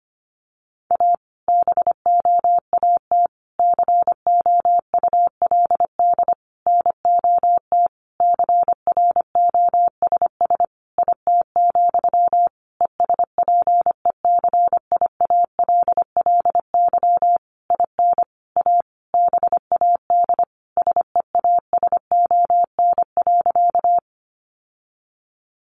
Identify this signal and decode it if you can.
Morse code, heard as a keyed tone, about 25 wpm, 700 Hz. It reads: A BOAT COULD NOT CROSS IT, ESPECIALLY IN A BAD SEASON.
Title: Sentences from Top 1400 Words 25wpm